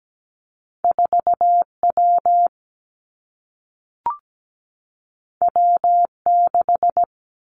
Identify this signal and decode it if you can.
Morse code, heard as a keyed tone, about 17 wpm, 700 Hz. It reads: 4 W W 6
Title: Two Challenging Letter-Number Combinations 17wpm